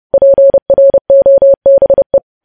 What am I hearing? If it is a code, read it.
Morse code, heard as a keyed tone, about 30 wpm, 550 Hz. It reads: PROBE
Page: News Headlines in Morse Code at 30 WPM